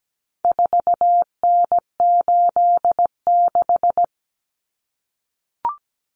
Morse code, heard as a keyed tone, about 17 wpm, 700 Hz. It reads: 4N86
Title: Common Cut Numbers - 4-Digit 17wpm